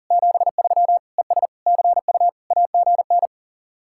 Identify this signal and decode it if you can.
Morse code, heard as a keyed tone, about 50 wpm, 700 Hz. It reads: 73 ES CU AGN